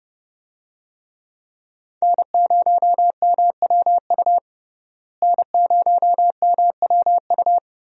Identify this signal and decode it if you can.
Morse code, heard as a keyed tone, about 29 wpm, 700 Hz. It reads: N0MWU N0MWU
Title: USA Call Signs - 5 Characters - Repeated 3x - 30wpm